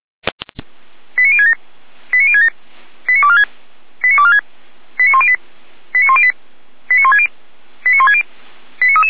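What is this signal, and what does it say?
Five-tone selective call (ZVEI), heard as quick runs of five tones, tones of about 80 ms. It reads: 89067 89067 89246 89246 89108 89108 89160 89160
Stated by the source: ZVEI five-tone selective call 89150 received on an SDR.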